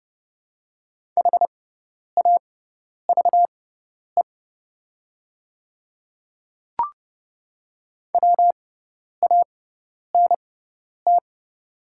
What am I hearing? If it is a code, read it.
Morse code, heard as a keyed tone, about 30 wpm, spaced as for 10 wpm, 690 Hz. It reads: HAVE WANT
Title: 6x Character Spacing - Top 300 Words - 30wpm